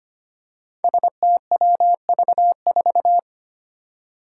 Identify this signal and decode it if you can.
Morse code, heard as a keyed tone, about 25 wpm, 700 Hz. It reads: STWV4
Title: Five Challenging Letter-Number Combinations 25wpm